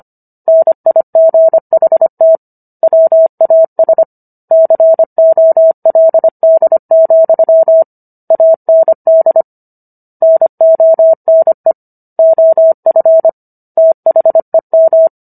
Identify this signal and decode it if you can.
Morse code, heard as a keyed tone, about 25 wpm, 650 Hz. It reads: NIGHT WAS COLD, AND NONE OF THEM
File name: Book One / 16-Chapter 16 - 25 WPM